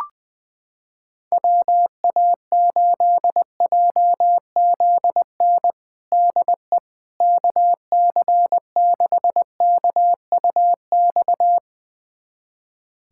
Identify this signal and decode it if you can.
Morse code, heard as a keyed tone, about 20 wpm, 700 Hz. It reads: WA8JZN DE KC6KUX
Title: USA Callsign DE Callsign - 6 Characters - 20wpm